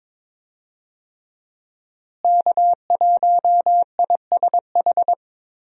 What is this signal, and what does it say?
Morse code, heard as a keyed tone, about 22 wpm, 700 Hz. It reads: K1ISH
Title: USA Call Signs - 5 Characters - Repeated 3x - 22wpm